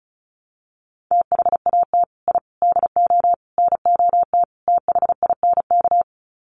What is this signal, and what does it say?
Morse code, heard as a keyed tone, about 34 wpm, 700 Hz. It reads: THAT I DO NOT THINK
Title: Sets of 5 Words Encore 35wpm